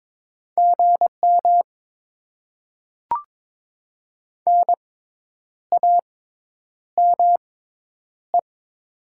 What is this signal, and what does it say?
Morse code, heard as a keyed tone, about 21 wpm, spaced as for 9 wpm, 700 Hz. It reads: GM NAME
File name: Top 20 QSO Elements - 6x Character Spacing - 22wpm